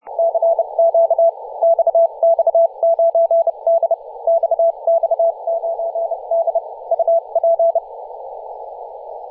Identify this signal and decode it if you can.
Morse code, heard as a keyed tone, about 30 wpm, spaced as for 26 wpm, 670 Hz. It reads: CQ XX9D XX9D UP
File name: posts / 2017-feb-16-0552z-xx9d-21-026mhz-cw